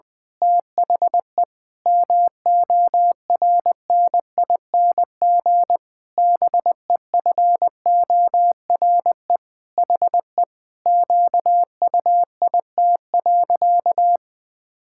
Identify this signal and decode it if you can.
Morse code, heard as a keyed tone, about 20 wpm, 700 Hz. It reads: THE MORNING BEFORE HE QUIT.
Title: Pangram Challenge v2 - 20wpm